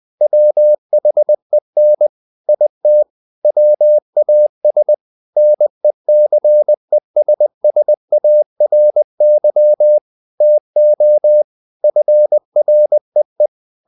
Morse code, heard as a keyed tone, about 20 wpm, 600 Hz. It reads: WHEN IT WAS NECESSARY TO FREE
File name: the old man and the sea / the old man and the sea part 1 20wpm